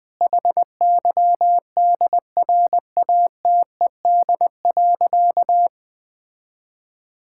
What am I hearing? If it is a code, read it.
Morse code, heard as a keyed tone, about 20 wpm, 700 Hz. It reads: HYDRATED.
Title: Difficult Sentences with Hints - 20wpm